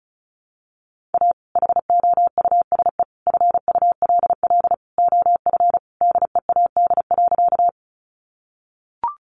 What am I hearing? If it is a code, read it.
Morse code, heard as a keyed tone, about 34 wpm, 700 Hz. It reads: A HOUSE FULL OF DEAD.
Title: Sentences from Top 1500 Words 35wpm